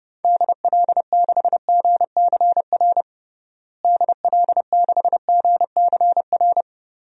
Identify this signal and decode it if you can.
Morse code, heard as a keyed tone, about 30 wpm, 700 Hz. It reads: DL6GCR DL6GCR
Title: DXCC-100 Call Signs - 6 Characters - Repeated 3x - 30wpm